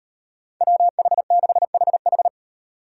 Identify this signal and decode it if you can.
Morse code, heard as a keyed tone, about 38 wpm, 700 Hz. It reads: WH6HH